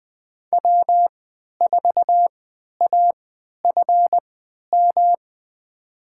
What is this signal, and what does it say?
Morse code, heard as a keyed tone, about 21 wpm, spaced as for 11 wpm, 700 Hz. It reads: W4AFM
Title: USA Call Signs - 5 Characters - 3x Character Spacing - 20wpm